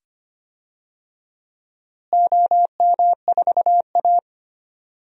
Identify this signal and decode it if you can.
Morse code, heard as a keyed tone, about 25 wpm, 700 Hz. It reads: OM4A
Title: DXCC-100 Call Signs - 4 Characters - Repeated 2x - 25wpm